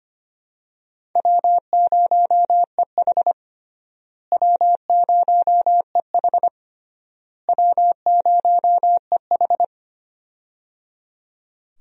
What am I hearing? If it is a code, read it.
Morse code, heard as a keyed tone, about 25 wpm, 700 Hz. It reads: W0EH W0EH W0EH
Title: USA Call Signs - 4 Characters - Repeated 3x - 25wpm